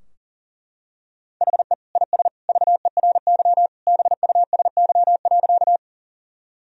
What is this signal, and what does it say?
Morse code, heard as a keyed tone, about 40 wpm, 700 Hz. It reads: HE IS VERY BUSY.
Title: Sentences from Top 1000 Words - Review Part 3 - 40wpm